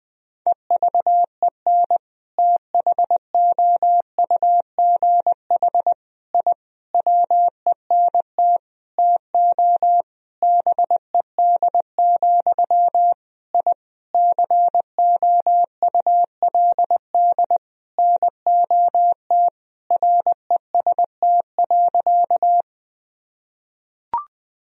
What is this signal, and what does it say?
Morse code, heard as a keyed tone, about 20 wpm, 700 Hz. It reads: EVEN THOUGH I WENT TO BED, I COULD NOT REST.
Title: Sentences from Top 1000 Words - Review Part 2 - 20wpm